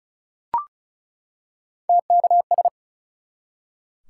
Morse code, heard as a keyed tone, about 36 wpm, 700 Hz. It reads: TKS
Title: Top 180 QSO Elements 35wpm